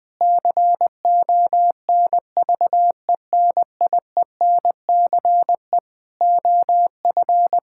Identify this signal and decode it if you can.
Morse code, heard as a keyed tone, about 20 wpm, 700 Hz. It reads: CONVENIENCE OF
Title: Difficult Sentences with Hints - 20wpm